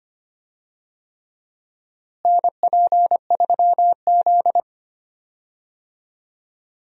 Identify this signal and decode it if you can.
Morse code, heard as a keyed tone, about 25 wpm, 700 Hz. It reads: NP3Z
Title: USA Call Signs - 4 Characters - Repeated 3x - 25wpm